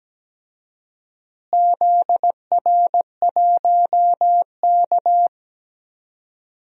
Morse code, heard as a keyed tone, about 17 wpm, 700 Hz. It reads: ZR1K
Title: DXCC-100 Call Signs - 4 Characters - Repeated 2x - 17wpm